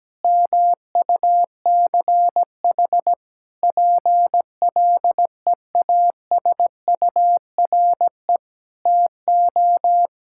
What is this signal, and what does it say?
Morse code, heard as a keyed tone, about 17 wpm, 700 Hz. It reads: MUCH PLEASURE TO
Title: Sentences from Top 1600 Words 17wpm